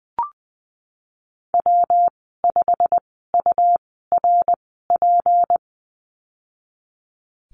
Morse code, heard as a keyed tone, about 20 wpm, spaced as for 15 wpm, 700 Hz. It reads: W5URP
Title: 161 Lesson 26g - Letter g - Call Signs 20wpm